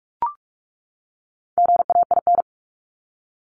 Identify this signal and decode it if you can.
Morse code, heard as a keyed tone, about 44 wpm, 700 Hz. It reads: GAIN
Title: Top 751-1000 Words 45wpm